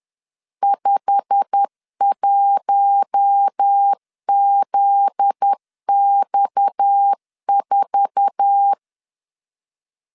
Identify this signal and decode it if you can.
Morse code, heard as a keyed tone, about 10 wpm, 800 Hz. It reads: 51ZX4
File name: LEZIONI / 8 YJ45 / 8 YJ45 10wpm 2549